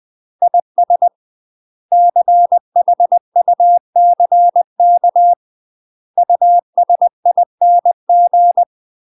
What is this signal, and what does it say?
Morse code, heard as a keyed tone, about 20 wpm, 700 Hz. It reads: IS CHUCK USING
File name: UK MASTER QSOs TRAINING / 20WPM / UK MASTER QSOs 0050